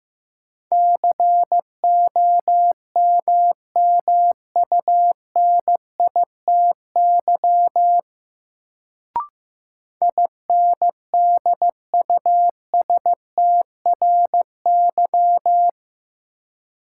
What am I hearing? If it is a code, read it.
Morse code, heard as a keyed tone, about 15 wpm, 700 Hz. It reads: COMMUNITY INDUSTRY
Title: Sentences from Top 1000 Words - Review Part 4 - New Word Drill - 15wpm